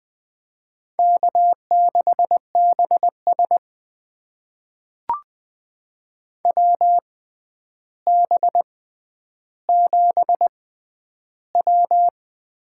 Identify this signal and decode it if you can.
Morse code, heard as a keyed tone, about 20 wpm, spaced as for 10 wpm, 700 Hz. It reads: K6BS WB7W
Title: USA Call Signs - 6x Character Spacing - 20wpm